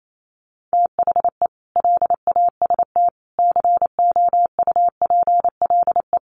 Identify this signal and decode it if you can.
Morse code, heard as a keyed tone, about 28 wpm, 700 Hz. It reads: THE LAST COUPLE
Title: Sets of 5 Words Encore2 28wpm